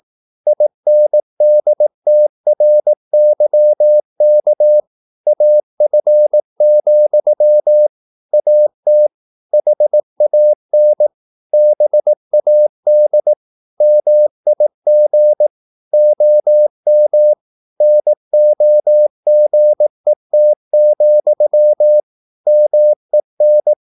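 Morse code, heard as a keyed tone, about 18 wpm, 600 Hz. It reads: INDTRYK AF, AT HAN BAD MIG OM NOGET, MEN
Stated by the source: Slow (QRS) Morse practice at 18 wpm, CW med Gnister.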